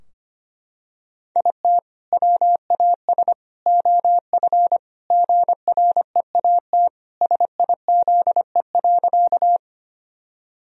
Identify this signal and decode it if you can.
Morse code, heard as a keyed tone, about 25 wpm, 700 Hz. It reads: IT WAS OF GREAT SIZE.
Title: Sentences from Top 1000 Words - Review Part 2 - 25wpm